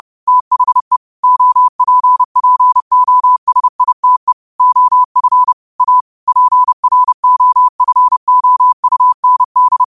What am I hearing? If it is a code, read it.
Morse code, heard as a keyed tone, about 30 wpm, 1000 Hz. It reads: THE OPPOSITE OF A PROFOUND
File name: qotdfiles / 30wpm